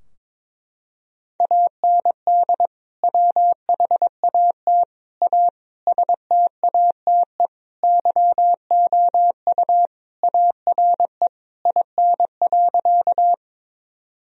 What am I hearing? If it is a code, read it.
Morse code, heard as a keyed tone, about 22 wpm, 700 Hz. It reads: AND WHAT A STATE YOU ARE IN.
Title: Sentences from Top 1000 Words - Review Part 2 - 22wpm